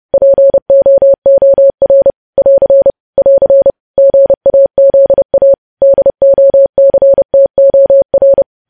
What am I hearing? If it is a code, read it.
Morse code, heard as a keyed tone, about 30 wpm, 550 Hz. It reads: POOR <AR> <AR> GAZA DOCTOR
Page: news headlines in morse code at 30 wpm